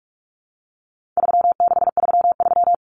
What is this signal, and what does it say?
Morse code, heard as a keyed tone, about 44 wpm, 700 Hz. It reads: 3633
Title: Common Cut Numbers - 4-Digit 45wpm